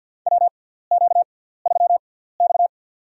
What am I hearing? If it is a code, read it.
Morse code, heard as a keyed tone, about 47 wpm, spaced as for 17 wpm, 700 Hz. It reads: WQ3X
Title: USA Call Signs - 4 Characters - 6x Character Spacing - 50wpm